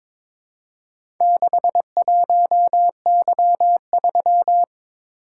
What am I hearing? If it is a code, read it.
Morse code, heard as a keyed tone, about 22 wpm, 700 Hz. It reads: 61Y3
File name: practice / 22wpm / Four Letter-Number Combinations - 22wpm